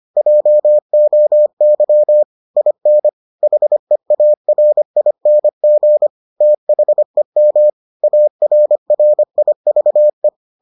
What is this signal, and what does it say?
Morse code, heard as a keyed tone, about 25 wpm, 600 Hz. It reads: JOY IN HEARING THEM ARRIVE